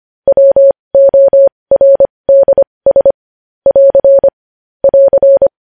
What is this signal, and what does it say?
Morse code, heard as a keyed tone, about 25 wpm, spaced as for 20 wpm, 550 Hz. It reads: WORDS <AR> <AR>
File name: podcast / News headlines-1x-25x20